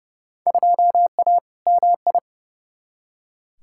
Morse code, heard as a keyed tone, about 30 wpm, 700 Hz. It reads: 2A MI
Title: Field Day Exchanges - 30wpm